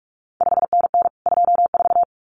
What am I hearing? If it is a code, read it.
Morse code, heard as a keyed tone, about 46 wpm, 700 Hz. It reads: 5NN 24
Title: CQ WW DX Contest Exchange Encore - 45wpm